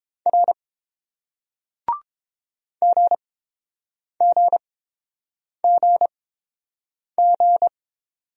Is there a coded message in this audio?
Yes: Morse code from 0.3 to 7.7 s, heard as a keyed tone, about 30 wpm, spaced as for 7 wpm, 700 Hz. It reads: R GGGG